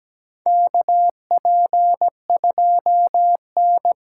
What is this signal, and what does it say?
Morse code, heard as a keyed tone, about 17 wpm, 700 Hz. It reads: KP2N
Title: K1USN SST QSO Practice - 17wpm